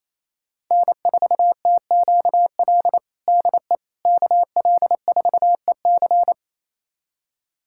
Morse code, heard as a keyed tone, about 28 wpm, 690 Hz. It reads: N4TQL DE KL4EC
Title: USA Callsign DE Callsign - 5 Characters - 28wpm